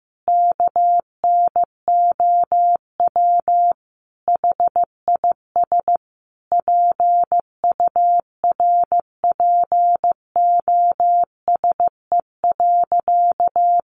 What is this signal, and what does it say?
Morse code, heard as a keyed tone, about 15 wpm, 700 Hz. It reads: KNOW HIS PURPOSE.
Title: Sentences from Top 800 Words 15wpm